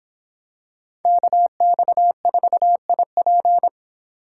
Morse code, heard as a keyed tone, about 25 wpm, 700 Hz. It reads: KX4IP